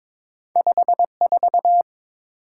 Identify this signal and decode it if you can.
Morse code, heard as a keyed tone, about 22 wpm, 700 Hz. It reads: 54
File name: practice / 22wpm / Two Letter-Number Combinations - 22wpm